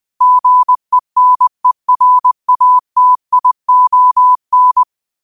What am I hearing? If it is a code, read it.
Morse code, heard as a keyed tone, about 20 wpm, 1000 Hz. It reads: GENERATION